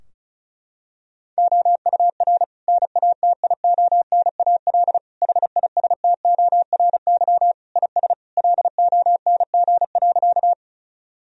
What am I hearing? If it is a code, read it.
Morse code, heard as a keyed tone, about 36 wpm, 700 Hz. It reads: OUR NATIONAL HISTORY IS LONG.
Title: Sentences from Top 1000 Words - Review Part 4 - 35wpm